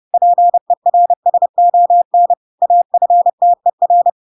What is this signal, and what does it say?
Morse code, heard as a keyed tone, about 30 wpm, 700 Hz. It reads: PERSON AFTER
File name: Pinocchio / Pinocchio ch3 30 30 ik1hge